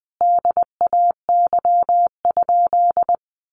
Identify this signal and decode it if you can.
Morse code, heard as a keyed tone, about 20 wpm, 700 Hz. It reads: DAY?
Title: Sentences from Top 1100 Words 20wpm